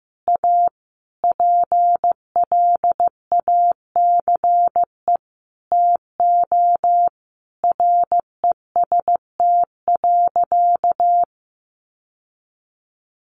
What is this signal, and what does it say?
Morse code, heard as a keyed tone, about 15 wpm, 700 Hz. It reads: A PLACE TO REST.
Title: Sentences from Top 600 Words 15wpm